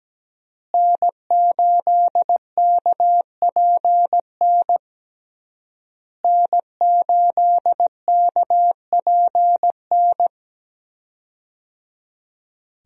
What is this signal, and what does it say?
Morse code, heard as a keyed tone, about 17 wpm, 700 Hz. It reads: N8KPN N8KPN
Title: USA Call Signs - 5 Characters - Repeated 3x - 17wpm